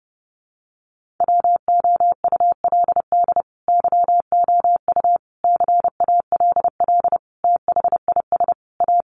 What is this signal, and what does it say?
Morse code, heard as a keyed tone, about 30 wpm, 700 Hz. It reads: WOULD YOU CALL THIS A
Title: Sentences from Top 600 Words 30wpm